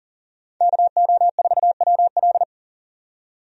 Morse code, heard as a keyed tone, about 37 wpm, 700 Hz. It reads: KO4WL
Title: USA Call Signs - 5 Characters - 40wpm